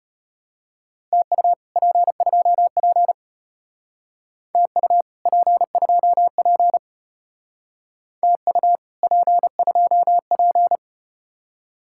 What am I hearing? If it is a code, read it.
Morse code, heard as a keyed tone, about 32 wpm, 700 Hz. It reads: TU P2P TU P2P TU P2P